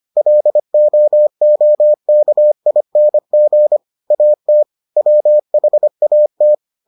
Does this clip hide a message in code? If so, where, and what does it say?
Yes: Morse code from 0.2 to 6.5 s, heard as a keyed tone, about 25 wpm, 600 Hz. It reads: LOOKING AT WHAT